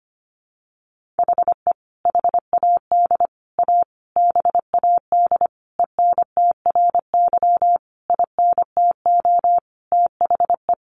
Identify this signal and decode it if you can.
Morse code, heard as a keyed tone, about 25 wpm, 700 Hz. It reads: HE HAD A BAD ENTRY INTO THE
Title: Sentences from Top 1500 Words 25wpm